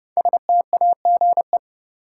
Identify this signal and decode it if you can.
Morse code, heard as a keyed tone, about 30 wpm, 700 Hz. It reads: STAGE
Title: Sentences from Top 1200 Words - New Word Drill Alt - 30wpm